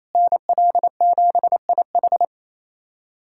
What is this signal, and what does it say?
Morse code, heard as a keyed tone, about 28 wpm, 700 Hz. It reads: NL7IH
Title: USA Callsign DE Callsign 28wpm